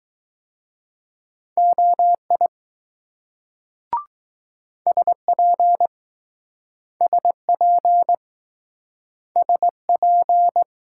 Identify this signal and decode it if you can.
Morse code, heard as a keyed tone, about 20 wpm, 700 Hz. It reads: OI SP SP SP